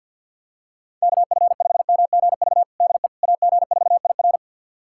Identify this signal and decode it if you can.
Morse code, heard as a keyed tone, about 50 wpm, 690 Hz. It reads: KF5MGV DE AG4IR
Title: USA Callsign DE Callsign 50wpm